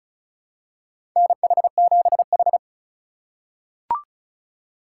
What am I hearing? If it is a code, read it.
Morse code, heard as a keyed tone, about 36 wpm, 700 Hz. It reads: NH7H
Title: USA Call Signs - 4 Characters - 2x Character Spacing - 35wpm